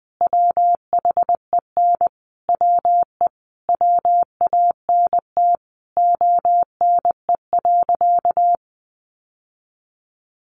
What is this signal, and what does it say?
Morse code, heard as a keyed tone, about 20 wpm, 700 Hz. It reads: WHEN WE WANT ONE.
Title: Sentences from Top 1200 Words 20wpm